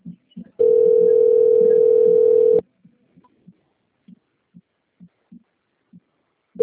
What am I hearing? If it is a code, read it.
Ringback tone (North America), 440 + 480 Hz.
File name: out- - - - - .85375